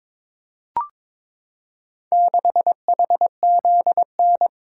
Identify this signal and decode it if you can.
Morse code, heard as a keyed tone, about 23 wpm, 700 Hz. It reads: 6HZN
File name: practice / 22wpm / Four Letter-Number Combinations - 22wpm